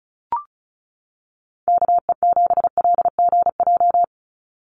Morse code, heard as a keyed tone, about 34 wpm, 700 Hz. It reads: KE7LGJ
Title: USA Call Signs 35wpm